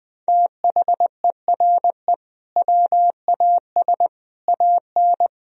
Morse code, heard as a keyed tone, about 21 wpm, 700 Hz. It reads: THERE WAS AN
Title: Sentences from Top 1000 Words - Review Part 4 - 20wpm